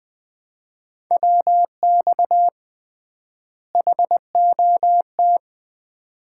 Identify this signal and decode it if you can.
Morse code, heard as a keyed tone, about 21 wpm, 700 Hz. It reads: WX HOT
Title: Common QSO Phrases - 3x Word-Spacing - 20wpm